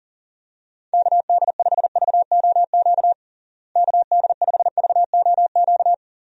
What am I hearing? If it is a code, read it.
Morse code, heard as a keyed tone, about 40 wpm, 700 Hz. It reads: KD5VOQ KD5VOQ
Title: USA Call Signs - 6 Characters - Repeated 2x - 40wpm